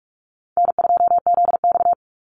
Morse code, heard as a keyed tone, about 44 wpm, 700 Hz. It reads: N2ZX
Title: Easy Call Signs Repeated 2x - 45wpm